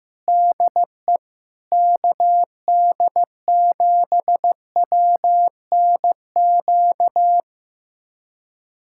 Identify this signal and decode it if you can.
Morse code, heard as a keyed tone, about 15 wpm, 700 Hz. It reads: DE KD7WNQ
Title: USA Callsign DE Callsign - 6 Characters - 15wpm